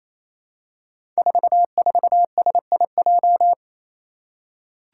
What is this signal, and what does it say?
Morse code, heard as a keyed tone, about 27 wpm, 700 Hz. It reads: 44SIJ
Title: Five Challenging Letter-Number Combinations 28wpm